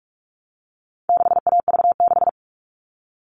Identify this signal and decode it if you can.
Morse code, heard as a keyed tone, about 44 wpm, 700 Hz. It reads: 6AV6
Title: All Cut Numbers - 4-Digit 45wpm